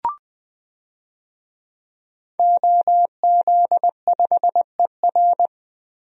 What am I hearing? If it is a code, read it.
Morse code, heard as a keyed tone, about 21 wpm, 700 Hz. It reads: OZ5ER
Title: DXCC-100 Call Signs - 5 Characters - Repeated 3x - 20wpm